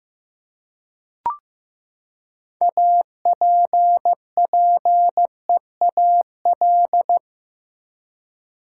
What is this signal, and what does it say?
Morse code, heard as a keyed tone, about 15 wpm, 700 Hz. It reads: APPEAL
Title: Six Letter Words - 15wpm